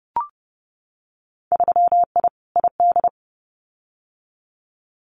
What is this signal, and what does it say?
Morse code, heard as a keyed tone, about 30 wpm, 700 Hz. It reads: 3I ID
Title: Winter Field Day Exchanges 30wpm